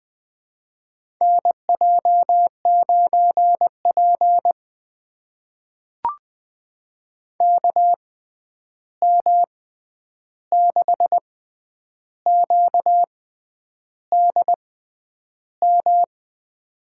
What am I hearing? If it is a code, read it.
Morse code, heard as a keyed tone, about 20 wpm, spaced as for 9 wpm, 700 Hz. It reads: NJ9P KM6QDM
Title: USA Call Signs - 6x Character Spacing - 20wpm